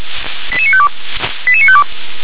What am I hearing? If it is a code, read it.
Five-tone selective call (ZVEI), heard as quick runs of five tones, tones of about 70 ms. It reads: 88952 88952